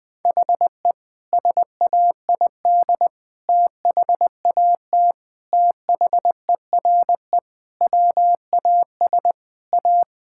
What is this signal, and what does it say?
Morse code, heard as a keyed tone, about 21 wpm, 700 Hz. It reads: HE SAID THAT THERE WAS A
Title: Sentences with 3-Word Phrases - 20wpm